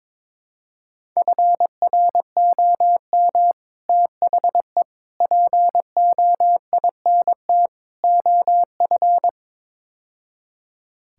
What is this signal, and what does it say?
Morse code, heard as a keyed tone, about 22 wpm, 700 Hz. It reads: FROM THE POINT OF
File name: practice / 22wpm / Build A Phrase - 5-Word Phrases - 22wpm